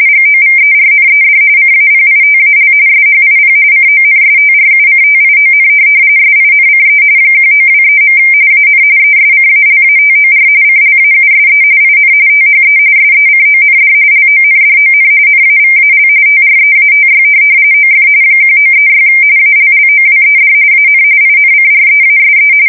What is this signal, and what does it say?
RTTY, about 45.45 baud. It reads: GOVERNED CAPITAL SHIP CONSTRUCTION SINCE 1923. AN ESCALATOR CLAUSE IN THE SECOND LONDON NAVAL TREATY OF 1936 ALLOWED AN INCREASE
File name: fsk170 / USS-Missouri